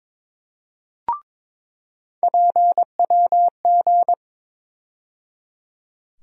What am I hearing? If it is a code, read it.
Morse code, heard as a keyed tone, about 22 wpm, 700 Hz. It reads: PWG
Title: USA Call Sign Postfixes - 3 Letters 22wpm